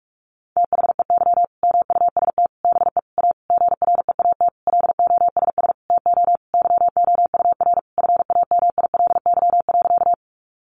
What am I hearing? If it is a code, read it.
Morse code, heard as a keyed tone, about 44 wpm, 700 Hz. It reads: THEY MUST BE A GREAT LOSS TO YOUR FAMILY.
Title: Sentences from Top 1400 Words 45wpm